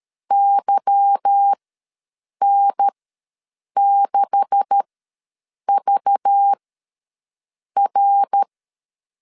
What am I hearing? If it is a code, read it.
Morse code, heard as a keyed tone, about 13 wpm, spaced as for 7 wpm, 800 Hz. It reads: YN6VR
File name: LEZIONI / 9 286 ws 12wpm 3351